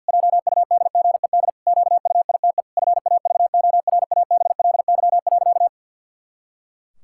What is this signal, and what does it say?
Morse code, heard as a keyed tone, about 50 wpm, 700 Hz. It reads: JUDGED QUITE FAVORABLY.
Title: Pangram Challenge v2 - 50wpm